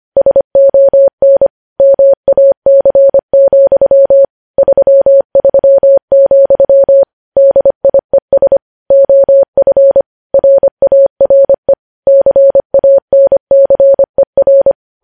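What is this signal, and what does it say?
Morse code, heard as a keyed tone, about 25 wpm, 550 Hz. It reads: SON MAC, 33, DIES OF RARE CANCER
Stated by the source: Morse code news headlines sent at 25 wpm.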